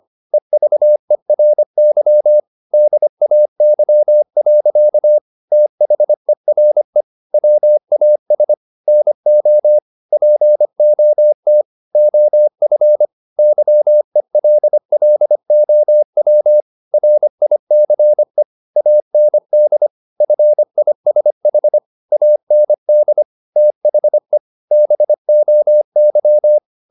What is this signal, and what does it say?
Morse code, heard as a keyed tone, about 25 wpm, 600 Hz. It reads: EVERY DAY. THERE WAS NO POT OF YELLOW RICE AND FISH AND THE BOY
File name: the old man and the sea part 1 25wpm